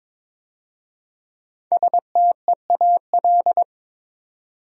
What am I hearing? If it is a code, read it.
Morse code, heard as a keyed tone, about 22 wpm, 700 Hz. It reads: STEAL
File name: 5 Letter Words - First 3 Letters Given - 22wpm